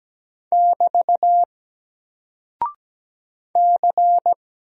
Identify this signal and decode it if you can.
Morse code, heard as a keyed tone, about 17 wpm, 700 Hz. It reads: <BT> C
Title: Single Character (Letter:Number:Punctuation:Prosign) Alt 17wpm